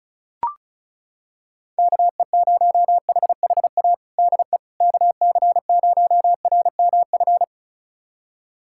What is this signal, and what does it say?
Morse code, heard as a keyed tone, about 36 wpm, 700 Hz. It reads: KE0HHA DE KC0RMF
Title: USA Callsign DE Callsign 35wpm